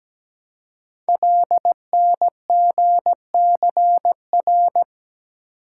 Morse code, heard as a keyed tone, about 17 wpm, 700 Hz. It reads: LNGCR
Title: Code Groups - 5 Letters - 17wpm